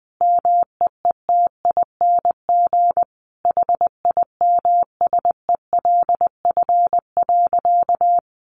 Morse code, heard as a keyed tone, about 20 wpm, 700 Hz. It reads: MEETING HIMSELF.
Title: A Sound Of Thunder 20wpm